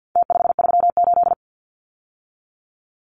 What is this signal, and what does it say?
Morse code, heard as a keyed tone, about 50 wpm, 700 Hz. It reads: T538
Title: Numbers - Contest Style - 4-Digit 50wpm